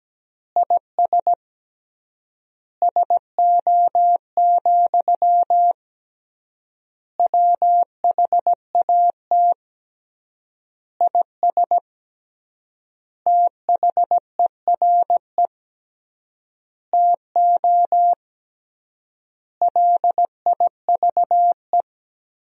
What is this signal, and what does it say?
Morse code, heard as a keyed tone, about 17 wpm, 700 Hz. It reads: IS SO, WHAT IS THERE TO LIVE
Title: 3x Word-Spacing - Sentences from Top 200 Words - 17wpm